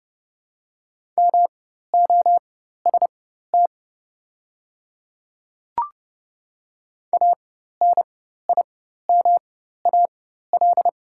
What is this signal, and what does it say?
Morse code, heard as a keyed tone, about 30 wpm, spaced as for 14 wpm, 700 Hz. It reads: MOST ANIMAL
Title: 4x Character Spacing - Top 500 Words - 30wpm